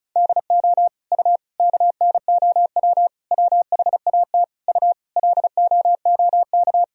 Morse code, heard as a keyed tone, about 36 wpm, 700 Hz. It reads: DO U KNOW WHAT U LOOK
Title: Sentences from Top 100 Words and Abbreviations 35wpm